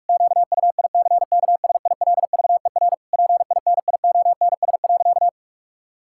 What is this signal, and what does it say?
Morse code, heard as a keyed tone, about 46 wpm, 700 Hz. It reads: QUICKSILVER PINIONS.
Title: Pangram Challenge v2 - 45wpm